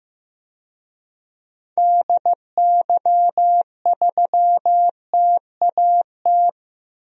Morse code, heard as a keyed tone, about 15 wpm, 700 Hz. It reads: DY3TAT
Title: DXCC-100 Call Signs - 6 Characters - 15wpm